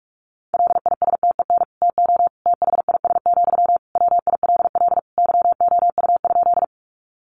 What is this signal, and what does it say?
Morse code, heard as a keyed tone, about 44 wpm, 700 Hz. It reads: LISTEN TO THIS, WILL YOU?
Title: Sentences from Top 400 Words 45wpm